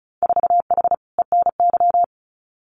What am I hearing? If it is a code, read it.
Morse code, heard as a keyed tone, about 36 wpm, 700 Hz. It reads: 4H ENY
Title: Winter Field Day Exchanges Encore 35wpm